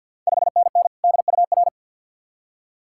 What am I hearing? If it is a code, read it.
Morse code, heard as a keyed tone, about 48 wpm, 690 Hz. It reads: 5NN DUR
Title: Ontario QSO Party Exchange 50wpm